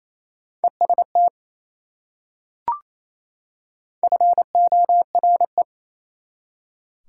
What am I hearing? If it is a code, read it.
Morse code, heard as a keyed tone, about 28 wpm, 700 Hz. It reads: EST FORE
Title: Prefixes and Suffixes 28wpm